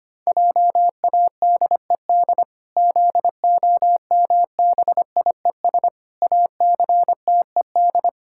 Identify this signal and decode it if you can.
Morse code, heard as a keyed tone, about 25 wpm, 700 Hz. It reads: JADED ZOMBIES ACTED
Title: Pangram Challenge v2 - 25wpm